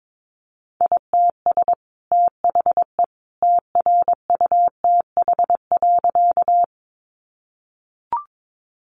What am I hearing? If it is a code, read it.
Morse code, heard as a keyed tone, about 22 wpm, 700 Hz. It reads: ITS THE TRUTH.
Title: Sentences from Top 800 Words 22wpm